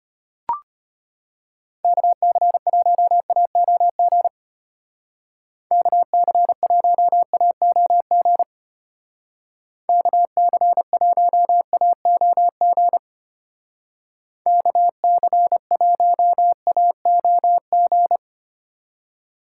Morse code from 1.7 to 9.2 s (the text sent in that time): KC1AOG KC1AOG